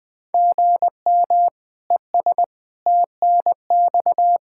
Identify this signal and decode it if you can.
Morse code, heard as a keyed tone, about 20 wpm, 700 Hz. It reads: GM ES TNX
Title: Practice QSO - Exchange 1 - Beginning - 20wpm